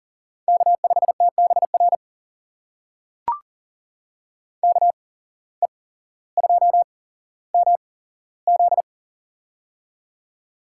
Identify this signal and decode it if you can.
Morse code, heard as a keyed tone, about 40 wpm, spaced as for 17 wpm, 700 Hz. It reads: K5TBR KE2MZ
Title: USA Call Signs - 5 Characters - 8x Character Spacing - 40wpm